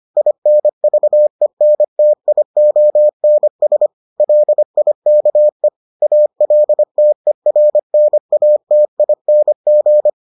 Morse code, heard as a keyed tone, about 25 wpm, 600 Hz. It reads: INVENTIONS LIKE ALTERNATING